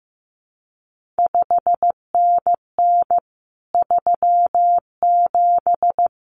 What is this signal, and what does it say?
Morse code, heard as a keyed tone, about 15 wpm, 700 Hz. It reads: 5NN 37
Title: CQ WW DX Contest Exchange Encore - 15wpm